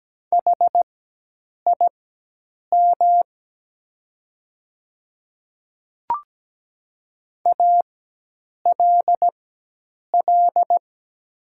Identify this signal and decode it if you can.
Morse code, heard as a keyed tone, about 17 wpm, spaced as for 8 wpm, 700 Hz. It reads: HIM ALL
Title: 4x Character Spacing - Top 100 Words - 17wpm